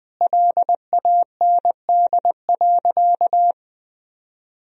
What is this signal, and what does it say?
Morse code, heard as a keyed tone, about 21 wpm, 700 Hz. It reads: LAND.
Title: Sentences from Top 1000 Words - Review Part 2 - 20wpm